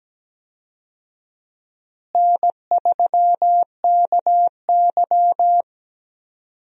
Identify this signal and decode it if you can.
Morse code, heard as a keyed tone, about 17 wpm, 700 Hz. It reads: N3KY
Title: USA Call Signs - 4 Characters - Repeated 2x - 17wpm